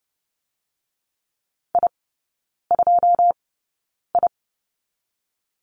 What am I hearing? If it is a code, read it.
Morse code, heard as a keyed tone, about 30 wpm, spaced as for 9 wpm, 690 Hz. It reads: I2I
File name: Lesson 17e - Letter u - Three Characters 30wpm